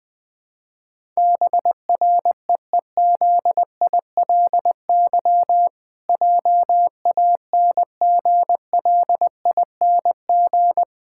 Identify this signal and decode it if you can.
Morse code, heard as a keyed tone, about 20 wpm, 700 Hz. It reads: BREEZILY JANGLING
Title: Pangram Challenge v2 - 20wpm